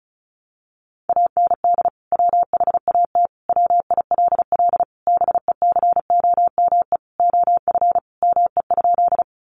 Morse code, heard as a keyed tone, about 34 wpm, 700 Hz. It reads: AND WHAT WILL BECOME OF ME?
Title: Sentences from Top 600 Words 35wpm